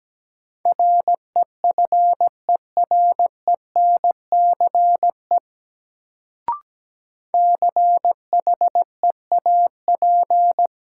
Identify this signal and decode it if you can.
Morse code, heard as a keyed tone, about 17 wpm, 700 Hz. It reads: REFERENCE CHEAP
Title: Sentences from Top 1200 Words - New Word Drill Alt - 17wpm